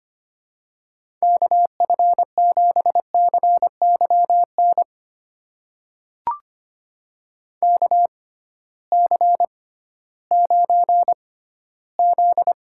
Morse code, heard as a keyed tone, about 25 wpm, spaced as for 14 wpm, 700 Hz. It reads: KF7CYN KC9Z
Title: USA Call Signs - 6 Characters - 6x Character Spacing - 25wpm